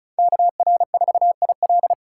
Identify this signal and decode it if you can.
Morse code, heard as a keyed tone, about 34 wpm, 700 Hz. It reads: KR4IL
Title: USA Call Signs - 5 Characters - 4x Character Spacing - 35wpm